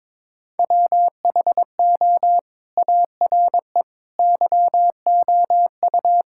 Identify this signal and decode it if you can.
Morse code, heard as a keyed tone, about 22 wpm, 700 Hz. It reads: WHO ARE YOU
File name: practice / 22wpm / Sentences with 2-Word Phrases - 22wpm